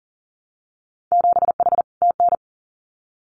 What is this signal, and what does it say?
Morse code, heard as a keyed tone, about 42 wpm, 700 Hz. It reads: 7H TN
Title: Winter Field Day Exchanges Encore 40wpm